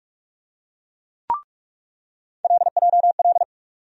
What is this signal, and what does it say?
Morse code, heard as a keyed tone, about 46 wpm, 700 Hz. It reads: LJL
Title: USA Call Sign Postfixes - 3 Letters 45wpm